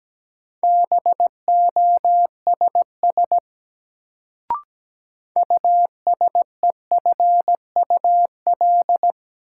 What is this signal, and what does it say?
Morse code, heard as a keyed tone, about 17 wpm, 700 Hz. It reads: BOSS USEFUL
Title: Sentences from Top 700 Words - New Word Drill - 17wpm